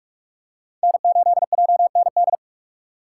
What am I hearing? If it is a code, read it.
Morse code, heard as a keyed tone, about 46 wpm, 690 Hz. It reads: N8JND
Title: USA Call Signs - 5 Characters - 6x Character Spacing - 45wpm